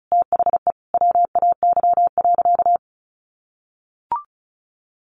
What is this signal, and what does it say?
Morse code, heard as a keyed tone, about 34 wpm, 700 Hz. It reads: THE WAY.
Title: Sentences from Top 400 Words 35wpm